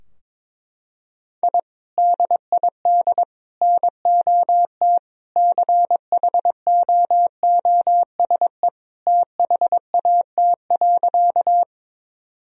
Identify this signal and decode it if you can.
Morse code, heard as a keyed tone, about 22 wpm, 700 Hz. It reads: I DID NOT CHOOSE THAT.
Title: Sentences from Top 1000 Words - Review Part 4 - 22wpm